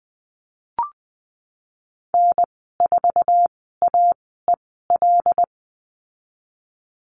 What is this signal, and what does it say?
Morse code, heard as a keyed tone, about 19 wpm, spaced as for 14 wpm, 700 Hz. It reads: N4AEL
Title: 128 Lesson 21g - Number 6 - Call Signs 20wpm